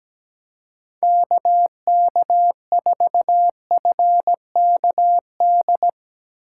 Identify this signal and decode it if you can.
Morse code, heard as a keyed tone, about 17 wpm, 700 Hz. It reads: KK4FKD
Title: USA Call Signs - 6 Characters - 2x Character Spacing - 17wpm